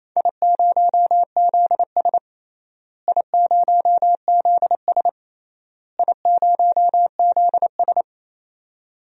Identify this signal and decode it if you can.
Morse code, heard as a keyed tone, about 28 wpm, 700 Hz. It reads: I0ZS I0ZS I0ZS
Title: DXCC-100 Call Signs - 4 Characters - Repeated 3x - 28wpm